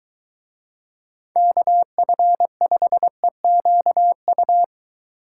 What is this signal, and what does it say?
Morse code, heard as a keyed tone, about 22 wpm, 700 Hz. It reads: KF5EQU